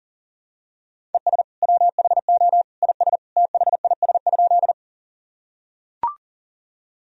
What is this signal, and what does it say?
Morse code, heard as a keyed tone, about 40 wpm, 700 Hz. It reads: ES WHO IS THIS?
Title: Sentences from Top 100 Words and Abbreviations 40wpm